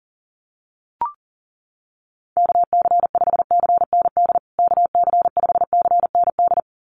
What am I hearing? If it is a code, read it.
Morse code, heard as a keyed tone, about 40 wpm, 700 Hz. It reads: KC5CND KC5CND
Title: USA Call Signs Repeated 2x 40wpm